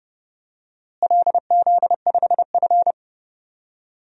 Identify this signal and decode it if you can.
Morse code, heard as a keyed tone, about 30 wpm, 700 Hz. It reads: LZ5F
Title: DXCC-100 Call Signs - 4 Characters - Repeated 3x - 30wpm